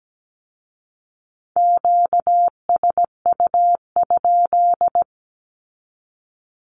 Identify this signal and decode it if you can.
Morse code, heard as a keyed tone, about 17 wpm, 700 Hz. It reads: QSU?
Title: Q-Codes with Spoken Meaning - 17wpm